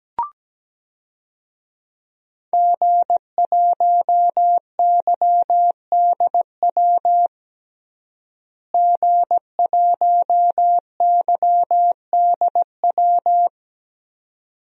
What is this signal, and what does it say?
Morse code, heard as a keyed tone, about 17 wpm, 700 Hz. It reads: G1YDW G1YDW
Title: DXCC-100 Call Signs - 5 Characters - Repeated 2x - 17wpm